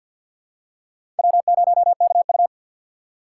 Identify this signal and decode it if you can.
Morse code, heard as a keyed tone, about 50 wpm, 700 Hz. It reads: W0KU
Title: USA Call Signs - 4 Characters - 3x Character Spacing - 50wpm